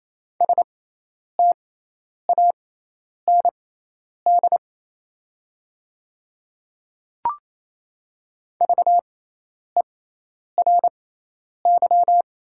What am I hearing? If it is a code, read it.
Morse code, heard as a keyed tone, about 28 wpm, spaced as for 10 wpm, 700 Hz. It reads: STAND VERY
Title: 6x Character Spacing - Top 300 Words - 28wpm